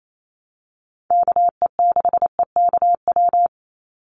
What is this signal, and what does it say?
Morse code, heard as a keyed tone, about 28 wpm, 700 Hz. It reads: KE6EKW
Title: USA Call Signs 28wpm